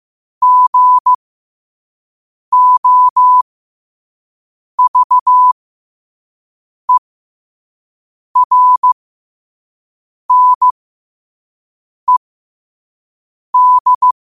Morse code, heard as a keyed tone, about 15 wpm, spaced as for 5 wpm, 1000 Hz. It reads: GOVERNED
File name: qotdfiles / 5wpm